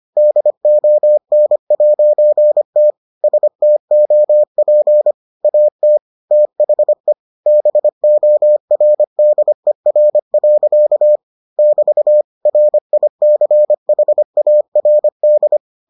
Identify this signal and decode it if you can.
Morse code, heard as a keyed tone, about 25 wpm, 600 Hz. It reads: DON'T STOP AT THE BORDER. <BT> RICHARD